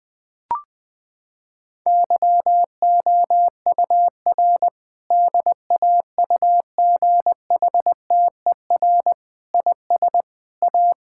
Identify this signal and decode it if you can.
Morse code, heard as a keyed tone, about 20 wpm, 700 Hz. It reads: YOUR DAUGHTER IS A
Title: Sentences from Top 1600 Words 20wpm